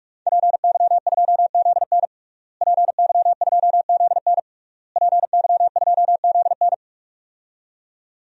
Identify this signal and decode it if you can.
Morse code, heard as a keyed tone, about 41 wpm, 700 Hz. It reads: PY2ZN PY2ZN PY2ZN
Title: DXCC-100 Call Signs - 5 Characters - Repeated 3x - 45wpm